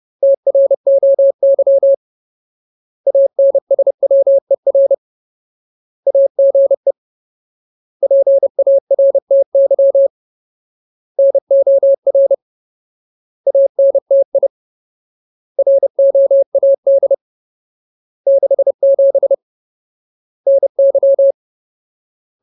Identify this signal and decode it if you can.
Morse code, heard as a keyed tone, about 30 wpm, 550 Hz. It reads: TROY ANSWER AGE PARTY NOR ANTI ROAD 67 NY
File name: Ultimate Word List / 30WPM Ultimate